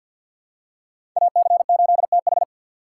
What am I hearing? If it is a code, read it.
Morse code, heard as a keyed tone, about 53 wpm, 700 Hz. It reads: AC8TH
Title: USA Call Signs - 4x Character Spacing - 50wpm